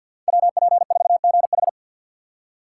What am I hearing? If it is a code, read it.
Morse code, heard as a keyed tone, about 49 wpm, 700 Hz. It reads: WP4GH
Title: USA Call Signs - 5 Characters - 6x Character Spacing - 50wpm